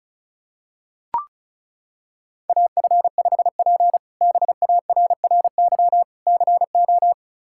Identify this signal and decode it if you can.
Morse code, heard as a keyed tone, about 34 wpm, 700 Hz. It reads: AF5P BARRY CO
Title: K1USN SST QSO Practice - 35wpm